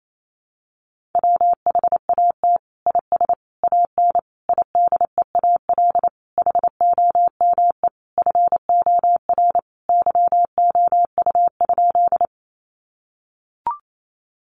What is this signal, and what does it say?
Morse code, heard as a keyed tone, about 28 wpm, 700 Hz. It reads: WHAT IS AN IDEAL HOME FOR YOU?
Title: Sentences from Top 1400 Words 28wpm